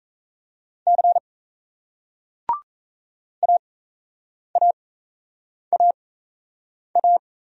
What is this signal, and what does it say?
Morse code, heard as a keyed tone, about 38 wpm, spaced as for 8 wpm, 700 Hz. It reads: C AAAA